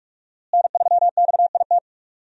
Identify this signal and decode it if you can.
Morse code, heard as a keyed tone, about 42 wpm, 700 Hz. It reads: N3XIT
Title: USA Call Signs - 5 Characters - Repeated 3x - 45wpm